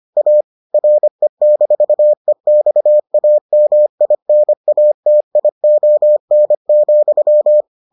Morse code, heard as a keyed tone, about 25 wpm, 600 Hz. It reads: A RE-EXAMINATION,